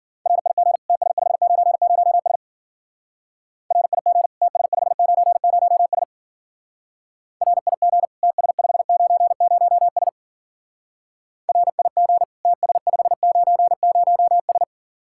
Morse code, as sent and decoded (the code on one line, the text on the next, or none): .-. .. --. / - ... ..... ----. ----- ... / .-. .. --. / - ... ..... ----. ----- ... / .-. .. --. / - ... ..... ----. ----- ... / .-. .. --. / - ... ..... ----. ----- ...
RIG TS590S RIG TS590S RIG TS590S RIG TS590S